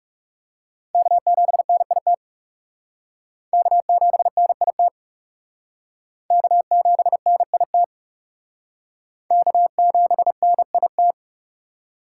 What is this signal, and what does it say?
Morse code, heard as a keyed tone, about 39 wpm, 700 Hz. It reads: K7NIT K7NIT K7NIT K7NIT